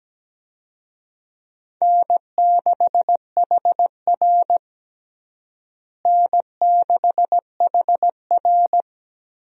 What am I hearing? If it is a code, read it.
Morse code, heard as a keyed tone, about 17 wpm, 700 Hz. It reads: N6HR N6HR
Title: USA Call Signs - 4 Characters - Repeated 2x - 17wpm